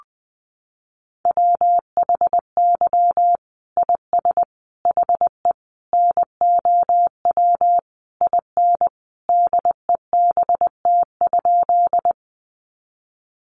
Morse code, heard as a keyed tone, about 20 wpm, 700 Hz. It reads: WHY IS HE NOW IN DEBT?
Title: Sentences from Top 1300 Words 20wpm